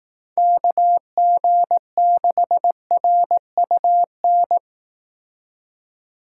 Morse code, heard as a keyed tone, about 18 wpm, 700 Hz. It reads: KG6RUN
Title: Speed Racing - USA 2x3 Call Signs 15wpm